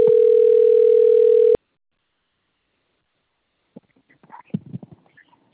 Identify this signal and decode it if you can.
Ringback tone (North America), 440 + 480 Hz.
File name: rg-1600-unknown- - - .61739